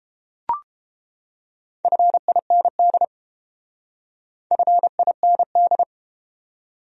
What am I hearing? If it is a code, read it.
Morse code, heard as a keyed tone, about 31 wpm, 700 Hz. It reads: FIND FIND